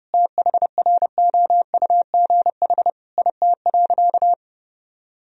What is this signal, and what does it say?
Morse code, heard as a keyed tone, about 32 wpm, 700 Hz. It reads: THROUGH IT.
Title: Sentences from Top 1000 Words - Review Part 2 - 30wpm